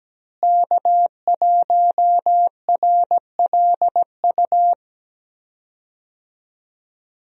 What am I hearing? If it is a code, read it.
Morse code, heard as a keyed tone, about 17 wpm, 700 Hz. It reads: K1RLU
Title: USA Call Signs - 5 Characters - Repeated 2x - 17wpm